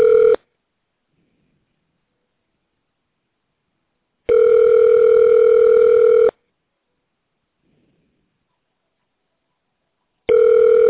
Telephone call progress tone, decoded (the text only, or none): ringback tone